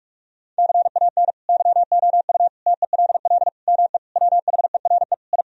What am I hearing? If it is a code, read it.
Morse code, heard as a keyed tone, about 46 wpm, 700 Hz. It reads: CAN YOU TELL ME WHERE I
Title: Sentences from Top 1000 Words - Review Part 2 - 45wpm